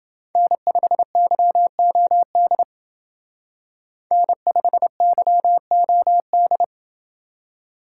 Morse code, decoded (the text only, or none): N5YOD N5YOD